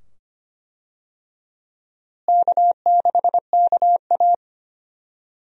Morse code, heard as a keyed tone, about 25 wpm, 700 Hz. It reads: K6KA
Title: USA Call Signs - 4 Characters - Repeated 2x - 25wpm